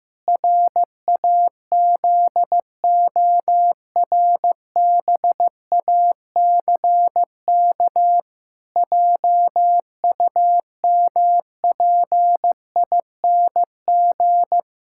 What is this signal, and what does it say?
Morse code, heard as a keyed tone, about 15 wpm, 700 Hz. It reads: RAZORBACK JUMPING
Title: Pangram Challenge v2 - 15wpm